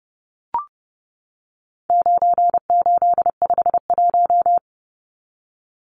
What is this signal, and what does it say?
Morse code, heard as a keyed tone, about 30 wpm, 700 Hz. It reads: 9851
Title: Numbers - 4-Digit 30wpm